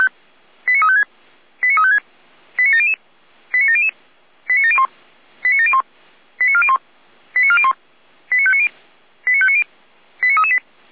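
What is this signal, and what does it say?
Five-tone selective call (ZVEI), heard as quick runs of five tones, tones of about 70 ms. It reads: 89356 89356 89700 89700 89701 89701 89401 89401 89590 89590 89338
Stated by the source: ZVEI five-tone call received on an SDR.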